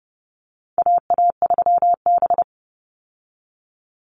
Morse code, heard as a keyed tone, about 30 wpm, 700 Hz. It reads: AA3B
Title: Call Signs Repeated 3x - 30wpm